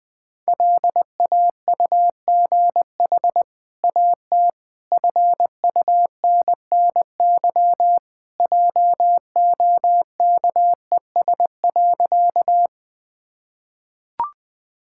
Morse code, heard as a keyed tone, about 19 wpm, 700 Hz. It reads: LAUGH AT FUNNY JOKES.
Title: Easy Sentences with Hints - 20wpm